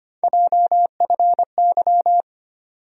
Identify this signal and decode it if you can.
Morse code, heard as a keyed tone, about 25 wpm, 700 Hz. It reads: JFY
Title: USA Call Sign Postfixes - 3 Letters 25wpm